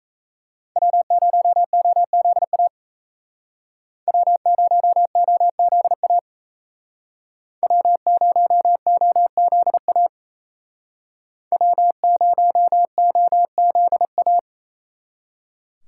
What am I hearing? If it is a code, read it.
Morse code, heard as a keyed tone, about 35 wpm, 700 Hz. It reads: W0OZA W0OZA W0OZA W0OZA
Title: Speed Racing - USA 1x3 Call Signs 28wpm